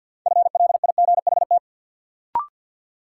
Morse code, heard as a keyed tone, about 52 wpm, 700 Hz. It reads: FLIGHT
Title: Sentences from Top 1000 Words - New Word Drill - 50wpm